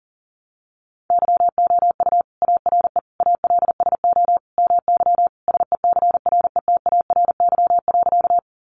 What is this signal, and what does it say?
Morse code, heard as a keyed tone, about 40 wpm, 700 Hz. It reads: YOU ARE ALSO MY SECRETARY.
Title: Sentences from Top 1300 Words 40wpm